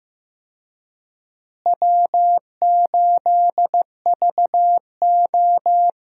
Morse code, heard as a keyed tone, about 15 wpm, 700 Hz. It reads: W8VO
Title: USA Call Signs - 4 Characters - Repeated 2x - 15wpm